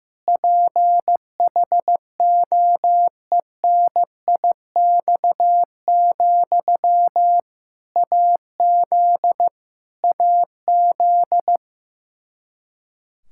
Morse code, heard as a keyed tone, about 15 wpm, 700 Hz. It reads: PHOENIX, AZ AZ
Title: K1USN SST QSO Practice 15wpm